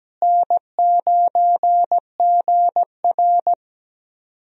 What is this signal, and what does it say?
Morse code, heard as a keyed tone, about 17 wpm, 700 Hz. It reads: N9GR
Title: USA Call Signs - 4 Characters - Repeated 3x - 17wpm